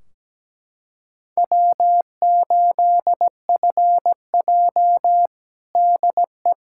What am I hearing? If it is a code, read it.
Morse code, heard as a keyed tone, about 17 wpm, 700 Hz. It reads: W8FJ DE
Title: Practice QSO - Exchange 4 - Ending - 17wpm